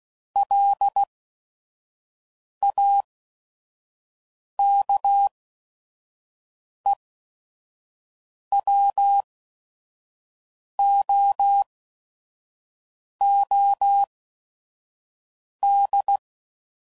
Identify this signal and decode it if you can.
Morse code, heard as a keyed tone, about 16 wpm, spaced as for 5 wpm, 800 Hz. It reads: LAKEWOOD
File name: CW Code Groups @5WPM / fromSM's / Cities